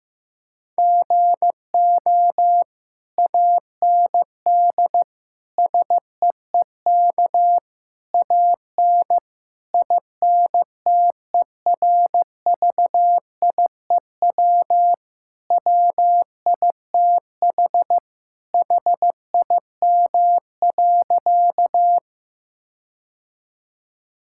Morse code, heard as a keyed tone, about 16 wpm, 700 Hz. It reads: GO AND SEEK AN INTERVIEW WITH HIM.
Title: Sentences from Top 1600 Words 15wpm